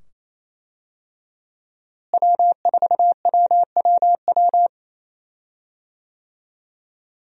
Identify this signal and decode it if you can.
Morse code, heard as a keyed tone, about 28 wpm, 700 Hz. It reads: W4WWW
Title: USA Call Signs - 5 Characters - Repeated 2x - 28wpm